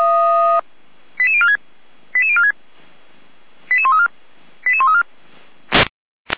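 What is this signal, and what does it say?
Five-tone selective call (ZVEI), heard as quick runs of five tones, tones of about 80 ms. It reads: 88046 88046 88134 88134
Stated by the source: ZVEI five-tone call 88335, received by SDR, fire brigade Orlamünde (feuerwehr-orlamuende).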